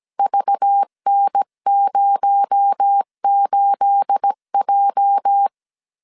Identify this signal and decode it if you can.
Morse code, heard as a keyed tone, about 17 wpm, 800 Hz. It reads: VN08J
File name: 1500C S B 16wpm